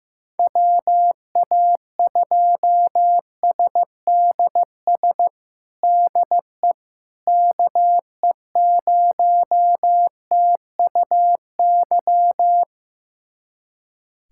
Morse code, heard as a keyed tone, about 15 wpm, 700 Hz. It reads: WA2SDS DE KE0TUY
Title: USA Callsign DE Callsign - 6 Characters - 15wpm